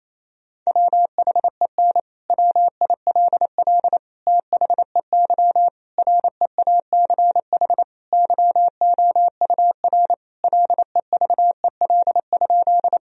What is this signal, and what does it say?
Morse code, heard as a keyed tone, about 28 wpm, 700 Hz. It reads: WHEN WILL THEY REACH YOUR LEVEL?
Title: Sentences from Top 1000 Words - Review Part 4 - 28wpm